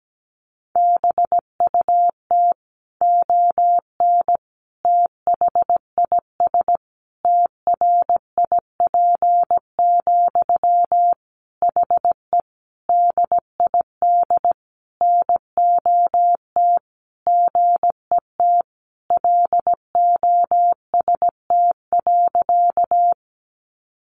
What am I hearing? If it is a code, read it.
Morse code, heard as a keyed tone, about 17 wpm, 700 Hz. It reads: BUT ON THIS TRIP, HE DID NOT GET LOST.
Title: Sentences from Top 1400 Words 17wpm